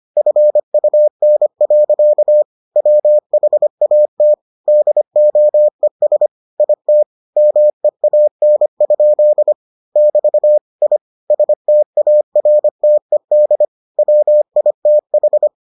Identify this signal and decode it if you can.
Morse code, heard as a keyed tone, about 25 wpm, 600 Hz. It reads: FUN. WHAT DOES IT MEAN? <BT> I STARTED WITH